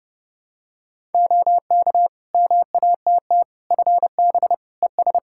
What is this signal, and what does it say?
Morse code, heard as a keyed tone, about 30 wpm, 700 Hz. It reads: OK MATT FB ES
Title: Practice QSO - Exchange 2 - Info - 30wpm